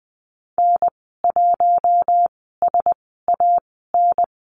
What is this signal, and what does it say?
Morse code, heard as a keyed tone, about 20 wpm, spaced as for 14 wpm, 700 Hz. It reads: N1SAN
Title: 069 Lesson 12g - Letter d - Call Signs 20wpm